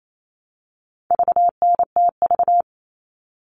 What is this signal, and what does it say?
Morse code, heard as a keyed tone, about 28 wpm, 700 Hz. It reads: VNTV
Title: All Cut Numbers - 4-Digit 28wpm